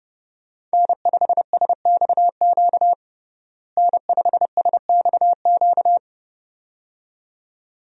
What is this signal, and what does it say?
Morse code, heard as a keyed tone, about 30 wpm, 700 Hz. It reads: N5SXQ N5SXQ
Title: USA Call Signs - 5 Characters - Repeated 2x - 30wpm